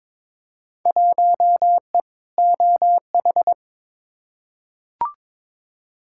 Morse code, heard as a keyed tone, about 22 wpm, 700 Hz. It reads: 1E OH
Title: Field Day Exchanges - 22wpm